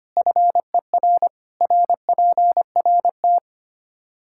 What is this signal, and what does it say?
Morse code, heard as a keyed tone, about 25 wpm, 700 Hz. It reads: FER RPRT
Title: K1USN SST QSO Practice 25wpm